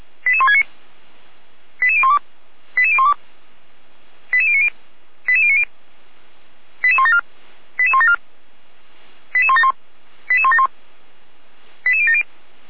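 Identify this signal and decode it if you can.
Five-tone selective call (ZVEI), heard as quick runs of five tones, tones of about 70 ms. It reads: 88169 88012 88012 88090 88090 88174 88174 88171 88171 88070
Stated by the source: ZVEI five-tone selective call 88191 received on an SDR.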